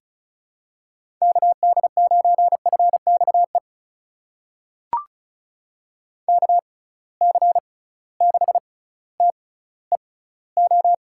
Morse code, heard as a keyed tone, about 34 wpm, spaced as for 18 wpm, 700 Hz. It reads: KD9FXE KC6TEO
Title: USA Call Signs - 6 Characters - 6x Character Spacing - 35wpm